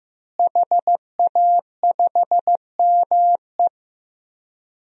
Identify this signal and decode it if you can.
Morse code, heard as a keyed tone, about 15 wpm, 700 Hz. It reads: HA5ME
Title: DXCC-100 Call Signs - 5 Characters - Repeated 3x - 15wpm